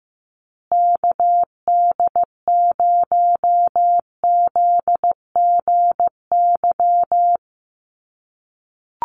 Morse code, heard as a keyed tone, about 15 wpm, 700 Hz. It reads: KD0ZGY
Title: USA Call Signs Repeated 3x 15wpm